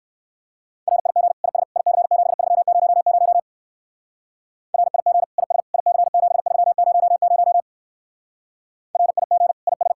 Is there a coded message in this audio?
Yes: Morse code from 0.9 to 9.9 s, heard as a keyed tone, about 56 wpm, 700 Hz. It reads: RIG IS IC7300 RIG IS IC7300 RIG IS